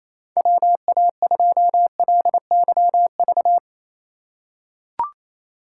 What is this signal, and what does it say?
Morse code, heard as a keyed tone, about 28 wpm, 700 Hz. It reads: WA2LYV
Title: USA Call Signs - 6 Characters - 3x Character Spacing - 28wpm